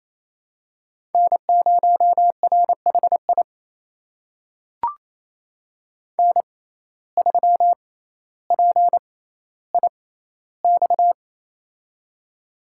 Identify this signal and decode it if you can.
Morse code, heard as a keyed tone, about 27 wpm, spaced as for 14 wpm, 700 Hz. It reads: N0RHI N3PIX
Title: USA Call Signs - 5 Characters - 6x Character Spacing - 28wpm